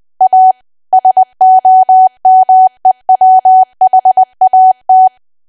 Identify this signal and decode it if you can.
Morse code, heard as a keyed tone, about 20 wpm, 750 Hz. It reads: A SOMEWHAT
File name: Archive / 20 WPM / 20WPM